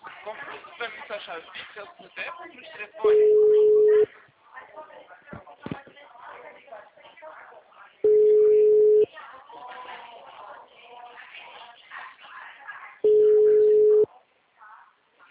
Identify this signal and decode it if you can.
Ringback tone (Europe, 425 Hz), with speech.
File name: out- -2002- - - .33524